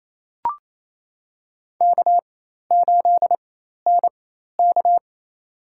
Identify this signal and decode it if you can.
Morse code, heard as a keyed tone, about 28 wpm, spaced as for 13 wpm, 700 Hz. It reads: K8NK
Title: USA Call Signs - 4x Character Spacing - 28wpm